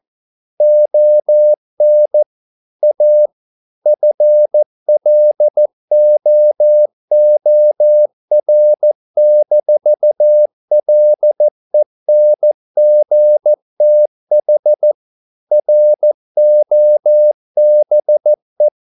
Morse code, heard as a keyed tone, about 14 wpm, 600 Hz. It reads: ON A FLOOR-LENGTH ROBE